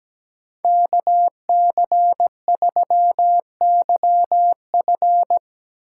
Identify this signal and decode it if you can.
Morse code, heard as a keyed tone, about 17 wpm, 700 Hz. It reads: KC3YF
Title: USA Call Signs - 3x Character Spacing - 17wpm